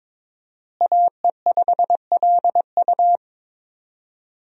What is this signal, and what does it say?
Morse code, heard as a keyed tone, about 22 wpm, 700 Hz. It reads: AE5LU
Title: USA Call Signs - 5 Characters - Repeated 3x - 22wpm